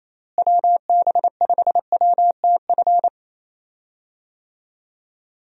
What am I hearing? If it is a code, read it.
Morse code, heard as a keyed tone, about 28 wpm, 700 Hz. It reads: WB5WTF
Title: USA Call Signs - 6 Characters - Repeated 2x - 28wpm